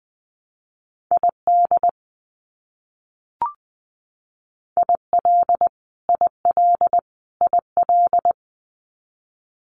Morse code, heard as a keyed tone, about 20 wpm, 700 Hz. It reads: ID IL IL IL
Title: US States Repeated 3x 20wpm